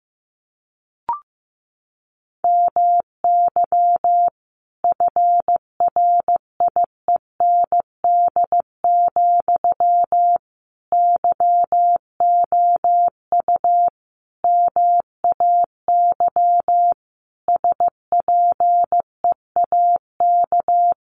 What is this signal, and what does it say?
Morse code, heard as a keyed tone, about 15 wpm, 700 Hz. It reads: MY FRIEND, YOU MAY SPEAK
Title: Sentences from Top 1300 Words 15wpm